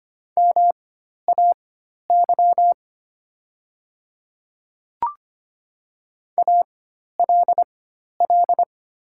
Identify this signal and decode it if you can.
Morse code, heard as a keyed tone, about 25 wpm, spaced as for 12 wpm, 700 Hz. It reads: MAY ALL
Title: 4x Character Spacing - Top 100 Words - 25wpm